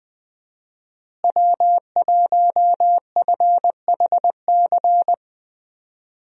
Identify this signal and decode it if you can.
Morse code, heard as a keyed tone, about 21 wpm, 700 Hz. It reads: W1FHC
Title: USA Call Signs - 5 Characters - Repeated 3x - 20wpm